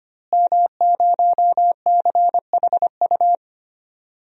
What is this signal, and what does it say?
Morse code, heard as a keyed tone, about 25 wpm, 700 Hz. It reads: M0CHU
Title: DXCC-100 Call Signs - 5 Characters - Repeated 2x - 25wpm